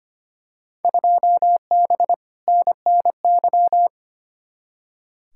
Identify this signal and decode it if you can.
Morse code, heard as a keyed tone, about 25 wpm, 700 Hz. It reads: 2B NNY
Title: Field Day Exchanges Encore - 25wpm